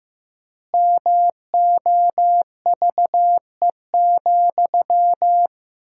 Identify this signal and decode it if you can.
Morse code, heard as a keyed tone, about 15 wpm, 700 Hz. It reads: MOVE,
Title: 4x Word-Spacing - Sentences from Top 300 Words - 15wpm